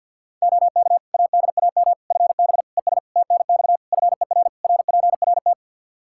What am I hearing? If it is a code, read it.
Morse code, heard as a keyed tone, about 51 wpm, 700 Hz. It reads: OK ADAM FB ES TNX FER RPRT
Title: Practice QSO - Exchange 2 - Info - 50wpm